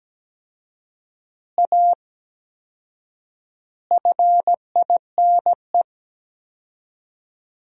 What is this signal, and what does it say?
Morse code, heard as a keyed tone, about 17 wpm, 700 Hz. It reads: A FINE
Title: 4x Word-Spacing - Sentences from Top 500 Words - 17wpm